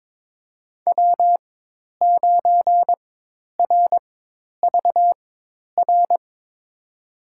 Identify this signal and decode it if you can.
Morse code, heard as a keyed tone, about 22 wpm, spaced as for 10 wpm, 700 Hz. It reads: W9RVR
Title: USA Call Signs - 5 Characters - 4x Character Spacing - 22wpm